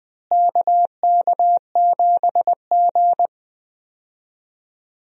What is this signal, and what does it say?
Morse code, heard as a keyed tone, about 21 wpm, 700 Hz. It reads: KK7G
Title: USA Call Signs - 4 Characters - Repeated 3x - 20wpm